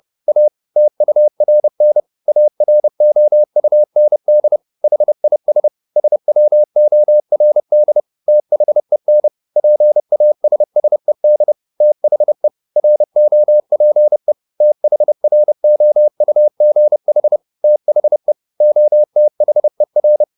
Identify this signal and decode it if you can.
Morse code, heard as a keyed tone, about 30 wpm, 600 Hz. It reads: A TURN AROUND HIS SWORD THEN PASSED THE ROPE THROUGH THE OTHER